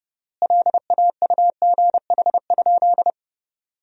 Morse code, heard as a keyed tone, about 30 wpm, 700 Hz. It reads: LAUGH?
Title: 3x Word-Spacing - Sentences from Top 500 Words - 30wpm